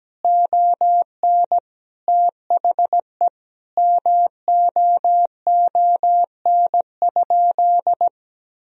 Morse code, heard as a keyed tone, about 17 wpm, 700 Hz. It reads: ON THE MOON?
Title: Sentences from Top 1000 Words - Review Part 3 - 17wpm